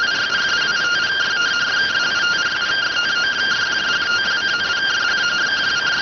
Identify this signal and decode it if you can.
Two-tone teleprinter signal, about 170 Hz shift.